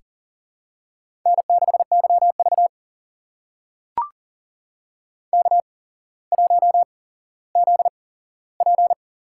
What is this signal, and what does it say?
Morse code, heard as a keyed tone, about 40 wpm, spaced as for 17 wpm, 700 Hz. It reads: N6YV K1ZP
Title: USA Call Signs - 4 Characters - 8x Character Spacing - 40wpm